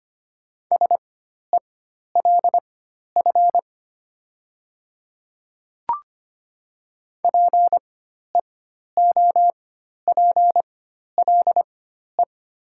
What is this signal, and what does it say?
Morse code, heard as a keyed tone, about 25 wpm, spaced as for 12 wpm, 700 Hz. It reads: SELF PEOPLE
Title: 4x Character Spacing - Top 500 Words - 25wpm